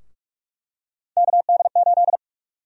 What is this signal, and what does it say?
Morse code, heard as a keyed tone, about 42 wpm, 700 Hz. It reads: KD8
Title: USA Call Sign Prefixes - Repeated 2x - 45wpm